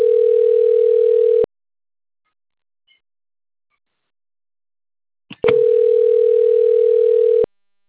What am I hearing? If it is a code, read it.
Ringback tone (North America), 440 + 480 Hz.